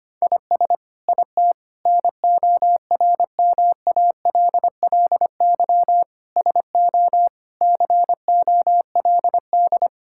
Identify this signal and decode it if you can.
Morse code, heard as a keyed tone, about 25 wpm, 700 Hz. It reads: IS IT NORMALLY SO COLD
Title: Sentences from Top 1000 Words - Review Part 6 - 25wpm